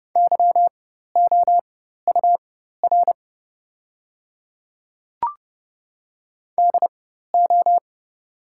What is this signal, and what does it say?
Morse code, heard as a keyed tone, about 30 wpm, spaced as for 14 wpm, 700 Hz. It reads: YOUR DO
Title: 4x Character Spacing - Top 100 Words - 30wpm